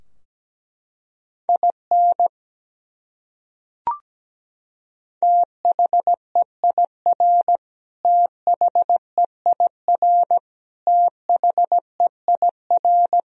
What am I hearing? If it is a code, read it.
Morse code, heard as a keyed tone, about 17 wpm, 700 Hz. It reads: IN THEIR THEIR THEIR
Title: Top 100 Words Repeated 17wpm